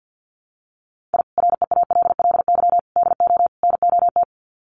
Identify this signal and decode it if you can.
Morse code, heard as a keyed tone, about 50 wpm, 700 Hz. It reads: I REALLY DO NOT
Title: Sets of 4 Words 50wpm